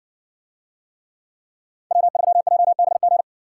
Morse code, heard as a keyed tone, about 55 wpm, 700 Hz. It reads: W3JBG